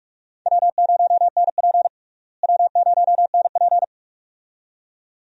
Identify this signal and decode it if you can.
Morse code, heard as a keyed tone, about 42 wpm, 700 Hz. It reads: W0NP W0NP
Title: USA Call Signs - 4 Characters - Repeated 2x - 45wpm